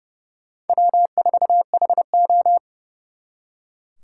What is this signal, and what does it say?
Morse code, heard as a keyed tone, about 37 wpm, 700 Hz. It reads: W4HO
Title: Speed Racing - USA 1x2 Call Signs 30wpm